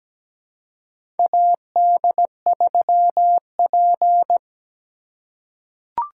Morse code, heard as a keyed tone, about 17 wpm, 700 Hz. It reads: AD3P
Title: USA Call Signs - 4 Characters - 2x Character Spacing - 17wpm